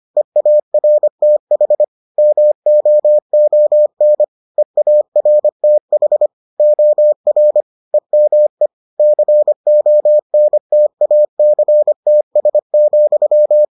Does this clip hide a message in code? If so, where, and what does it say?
Yes: Morse code from 0.2 to 13.6 s, heard as a keyed tone, about 25 wpm, 600 Hz. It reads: EARTH MOON EARTH OR EME CONTACTS,